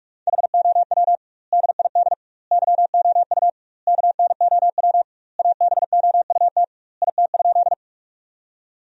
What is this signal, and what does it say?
Morse code, heard as a keyed tone, about 46 wpm, 700 Hz. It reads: HOW DID YOU KNOW ABOUT IT?
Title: 2x Word-Spacing - Sentences from Top 100 Words - 45wpm